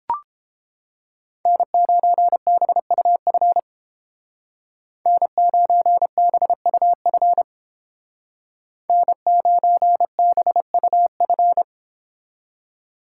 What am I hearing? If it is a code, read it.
Morse code, heard as a keyed tone, about 30 wpm, 700 Hz. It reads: N9BUF N9BUF N9BUF